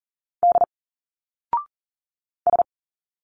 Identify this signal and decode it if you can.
Morse code, heard as a keyed tone, about 40 wpm, 700 Hz. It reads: D S
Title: Single Letters 40wpm